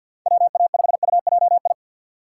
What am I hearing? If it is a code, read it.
Morse code, heard as a keyed tone, about 49 wpm, 700 Hz. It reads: WA5UJI
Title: USA Call Signs - 6 Characters - 4x Character Spacing - 50wpm